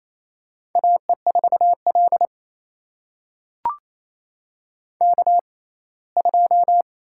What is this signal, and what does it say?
Morse code, heard as a keyed tone, about 28 wpm, spaced as for 19 wpm, 700 Hz. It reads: AE4L K2
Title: USA Call Signs - 6x Character Spacing - 28wpm